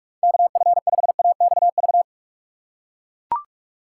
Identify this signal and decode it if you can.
Morse code, heard as a keyed tone, about 42 wpm, 700 Hz. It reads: KF5AXV
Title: USA Call Signs - 2x Character Spacing - 45wpm